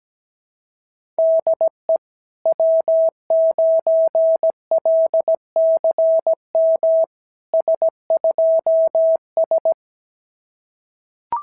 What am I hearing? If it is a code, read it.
Morse code, heard as a keyed tone, about 17 wpm, 650 Hz. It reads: DE W9LCM S2S
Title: SOTA QSO Practice 17wpm